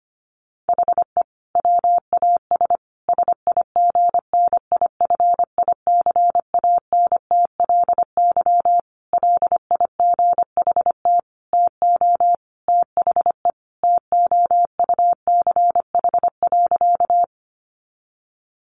Morse code, heard as a keyed tone, about 25 wpm, 700 Hz. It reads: HE WAS SIGNIFICANTLY LIGHT TO THE TOUCH.
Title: Sentences from Top 1500 Words 25wpm